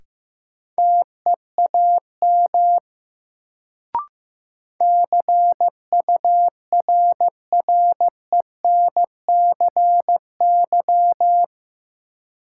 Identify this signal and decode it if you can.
Morse code, heard as a keyed tone, about 16 wpm, 700 Hz. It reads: TEAM CURRENCY
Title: Sentences from Top 1500 Words - New Word Drill - 15wpm